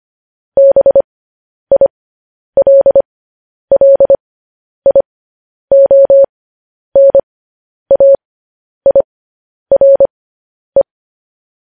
Morse code, heard as a keyed tone, about 25 wpm, spaced as for 10 wpm, 550 Hz. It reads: BILLIONAIRE